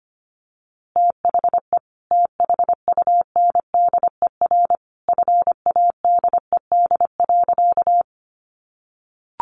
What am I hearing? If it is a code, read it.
Morse code, heard as a keyed tone, about 25 wpm, 700 Hz. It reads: THE THUNDER FADED.
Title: A Sound Of Thunder 25wpm